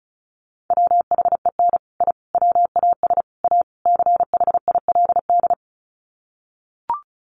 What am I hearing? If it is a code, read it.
Morse code, heard as a keyed tone, about 34 wpm, 700 Hz. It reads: WHEN I WAS A CHILD
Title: Sets of 5 Words Encore 35wpm